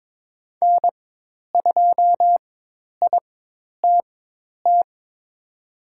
Morse code, heard as a keyed tone, about 22 wpm, spaced as for 10 wpm, 700 Hz. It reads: N2ITT
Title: USA Call Signs - 5 Characters - 4x Character Spacing - 22wpm